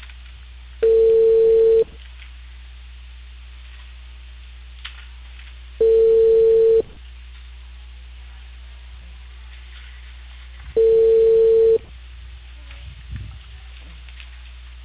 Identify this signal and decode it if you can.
Ringback tone (China, 450 Hz).